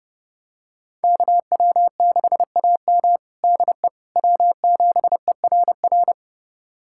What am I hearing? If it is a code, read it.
Morse code, heard as a keyed tone, about 32 wpm, 700 Hz. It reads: KW6AM DE W7ERR
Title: USA Callsign DE Callsign - 5 Characters - 30wpm